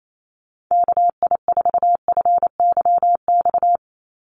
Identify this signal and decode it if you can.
Morse code, heard as a keyed tone, about 28 wpm, 700 Hz. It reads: KI4FYX
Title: USA Call Signs 28wpm